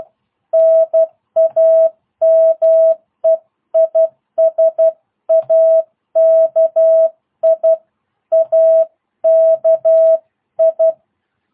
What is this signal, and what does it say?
Morse code, heard as a keyed tone, about 12 wpm, 650 Hz. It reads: NAME IS AKI AKI